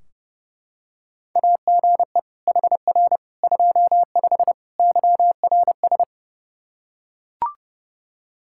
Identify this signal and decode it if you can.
Morse code, heard as a keyed tone, about 32 wpm, 700 Hz. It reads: AGE HR 25 YRS
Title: K1USN SST QSO Practice 30wpm